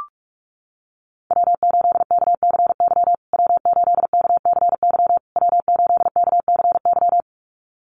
Morse code, heard as a keyed tone, about 44 wpm, 690 Hz. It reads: W8KCY W8KCY W8KCY
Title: USA Call Signs Repeated 3x 45wpm